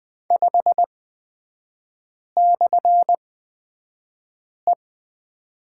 Morse code, heard as a keyed tone, about 20 wpm, spaced as for 5 wpm, 700 Hz. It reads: 5/E
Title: 236 Lesson 38e - Letter q - Three Characters 20wpm